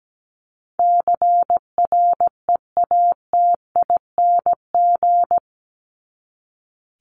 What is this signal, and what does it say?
Morse code, heard as a keyed tone, about 17 wpm, 700 Hz. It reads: CREATING
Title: ING Suffix - Top 100 Words with Base Words 17wpm